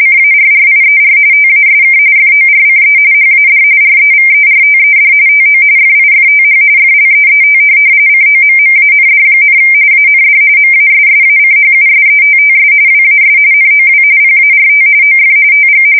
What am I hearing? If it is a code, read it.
RTTY, about 45.45 baud. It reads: CARRIER TASK FORCE HAD REVERTED TO FIFTH FLEET COMMAND, AND MASSACHUSETTS WAS NOW PART OF TG